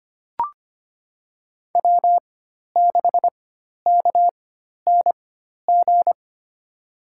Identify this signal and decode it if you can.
Morse code, heard as a keyed tone, about 25 wpm, spaced as for 12 wpm, 700 Hz. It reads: W6KNG
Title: USA Call Signs - 5 Characters - 4x Character Spacing - 25wpm